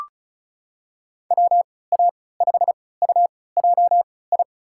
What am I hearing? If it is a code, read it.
Morse code, heard as a keyed tone, about 36 wpm, spaced as for 20 wpm, 700 Hz. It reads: WA5UJI
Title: USA Call Signs - 3x Character Spacing - 35wpm